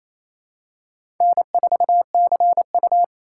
Morse code, heard as a keyed tone, about 28 wpm, 700 Hz. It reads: N4CU
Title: USA Call Signs - 4 Characters - 4x Character Spacing - 28wpm